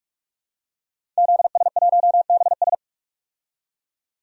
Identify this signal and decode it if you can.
Morse code, heard as a keyed tone, about 44 wpm, 700 Hz. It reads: ZS1BS
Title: DXCC-100 Call Signs - 5 Characters - 45wpm